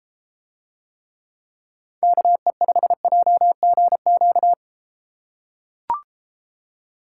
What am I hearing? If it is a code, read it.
Morse code, heard as a keyed tone, about 33 wpm, 700 Hz. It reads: KE5JGQ